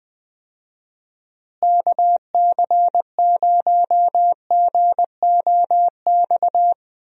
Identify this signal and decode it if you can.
Morse code, heard as a keyed tone, about 21 wpm, 700 Hz. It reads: KC0GOX
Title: USA Call Signs - 6 Characters - Repeated 3x - 20wpm